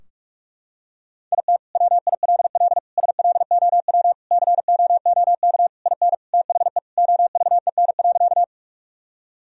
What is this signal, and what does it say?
Morse code, heard as a keyed tone, about 41 wpm, 700 Hz. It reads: IT WILL SLOW COOK IN THE OVEN.
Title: Sentences from Top 1000 Words - Review Part 4 - 45wpm